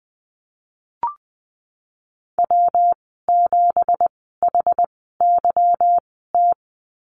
Morse code, heard as a keyed tone, about 19 wpm, spaced as for 14 wpm, 700 Hz. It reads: W7HYT
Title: 167 Lesson 27g - Number 7 - Call Signs 20wpm